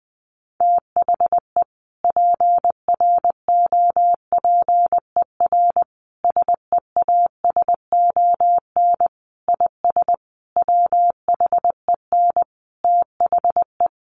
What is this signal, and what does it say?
Morse code, heard as a keyed tone, about 20 wpm, 700 Hz. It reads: THE PROPER SEASON IS WHEN THE
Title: Sentences from Top 1400 Words 20wpm